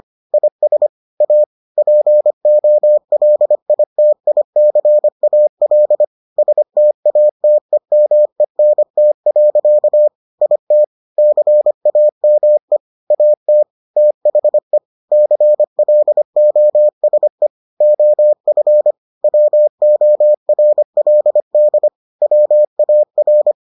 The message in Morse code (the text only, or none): IS A POLITICAL STATEMENT. IT CAME AT THE CLOSE OF WORLD WAR